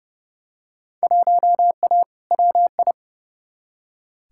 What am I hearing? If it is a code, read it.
Morse code, heard as a keyed tone, about 32 wpm, 700 Hz. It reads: 1A WI
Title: Field Day Exchanges - 30wpm